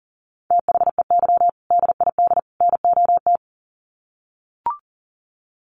Morse code, heard as a keyed tone, about 40 wpm, 700 Hz. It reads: THEY DID NOT
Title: Sets of 3 Words 40wpm